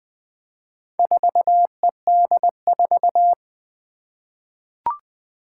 Morse code, heard as a keyed tone, about 20 wpm, 700 Hz. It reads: 4ED4
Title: Four Challenging Letter-Number Combinations 20wpm